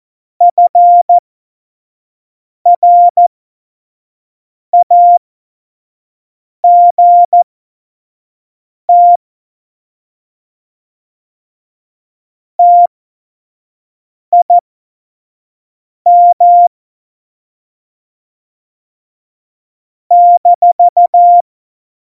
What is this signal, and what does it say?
Morse code, heard as a keyed tone, about 14 wpm, spaced as for 5 wpm, 700 Hz. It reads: FRAGT TIM -
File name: Morsekurs / Witze / Witz 07. Mai 2019, 25 F70